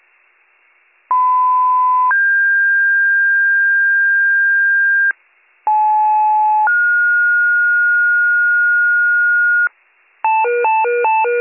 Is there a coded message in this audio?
Yes: two-tone page from 1.1 to 9.7 s, heard as a one-second tone, then a three-second tone, about 1010 then 1600 Hz, about 860 then 1430 Hz, with noise.